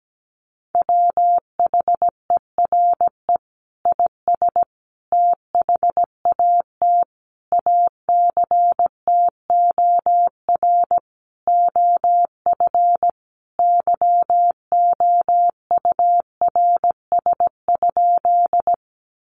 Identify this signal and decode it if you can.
Morse code, heard as a keyed tone, about 17 wpm, 700 Hz. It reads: WHERE IS THAT ACTOR OF YOURS?
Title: Sentences from Top 1500 Words 17wpm